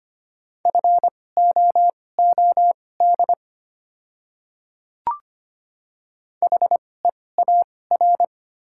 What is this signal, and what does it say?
Morse code, heard as a keyed tone, about 25 wpm, spaced as for 18 wpm, 690 Hz. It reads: FOOD HEAR
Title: 2x Character Spacing - Top 500 Words - 25wpm